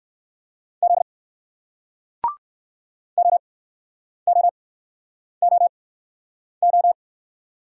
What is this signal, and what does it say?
Morse code, heard as a keyed tone, about 58 wpm, spaced as for 9 wpm, 700 Hz. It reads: Z OOOO